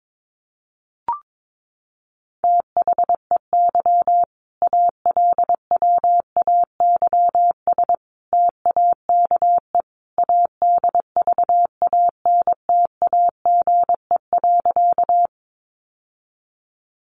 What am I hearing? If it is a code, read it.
Morse code, heard as a keyed tone, about 22 wpm, 700 Hz. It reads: THEY ALWAYS TAKE ADVANTAGE.
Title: Sentences from Top 900 Words 22wpm